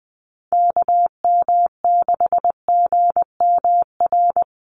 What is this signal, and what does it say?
Morse code, heard as a keyed tone, about 20 wpm, 700 Hz. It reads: KM6GMR
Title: USA Call Signs 20wpm